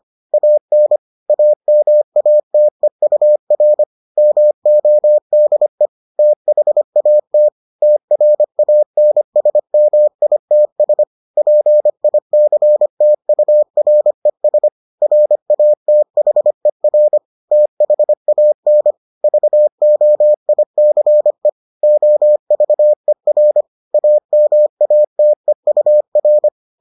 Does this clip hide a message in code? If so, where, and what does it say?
Yes: Morse code from 0.3 to 26.5 s, heard as a keyed tone, about 25 wpm, 600 Hz. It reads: AN AMATEUR MODE THAT TRANSMITS PICTURES RATHER THAN VOICE OVER AMATEUR